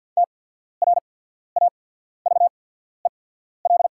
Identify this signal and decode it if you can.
Morse code, heard as a keyed tone, about 48 wpm, spaced as for 14 wpm, 690 Hz. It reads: TRAVEL
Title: 8x Character Spacing - Top 500 Words - 50wpm